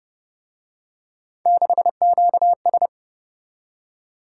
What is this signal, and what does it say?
Morse code, heard as a keyed tone, about 33 wpm, 700 Hz. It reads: 6QS